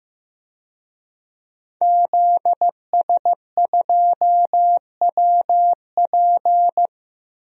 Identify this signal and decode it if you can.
Morse code, heard as a keyed tone, about 15 wpm, 700 Hz. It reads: ZS2WP
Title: DXCC-100 Call Signs - 5 Characters - Repeated 3x - 15wpm